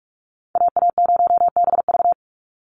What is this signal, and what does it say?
Morse code, heard as a keyed tone, about 44 wpm, 700 Hz. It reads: AA0BV
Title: NA SOTA Activator Callsigns 45wpm